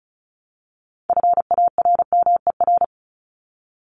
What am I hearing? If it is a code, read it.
Morse code, heard as a keyed tone, about 34 wpm, 700 Hz. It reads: FARMER
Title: Top 1751-2000 Words 35wpm